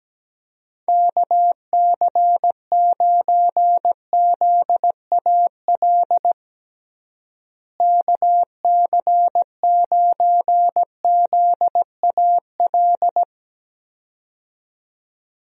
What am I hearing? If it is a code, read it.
Morse code, heard as a keyed tone, about 17 wpm, 700 Hz. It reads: KC9ZAL KC9ZAL
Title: USA Call Signs - 6 Characters - Repeated 2x - 17wpm